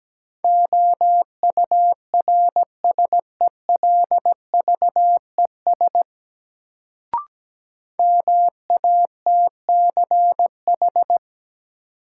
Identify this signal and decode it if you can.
Morse code, heard as a keyed tone, about 17 wpm, 700 Hz. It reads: OURSELVES MATCH
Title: Sentences from Top 1600 Words - New Word Drill Alt - 17wpm